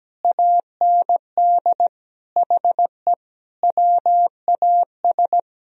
Morse code, heard as a keyed tone, about 17 wpm, 700 Hz. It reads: AND HE WAS
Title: Sentences from Top 1000 Words - Review Part 3 - 17wpm